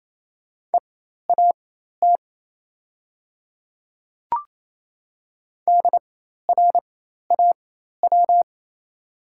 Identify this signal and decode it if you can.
Morse code, heard as a keyed tone, about 27 wpm, spaced as for 13 wpm, 690 Hz. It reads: EAT DRAW
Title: 4x Character Spacing - Top 300 Words - 28wpm